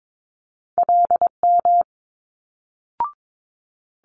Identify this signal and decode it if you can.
Morse code, heard as a keyed tone, about 22 wpm, 700 Hz. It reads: LM
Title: Multi-letter Phonemes Quick Fire 22wpm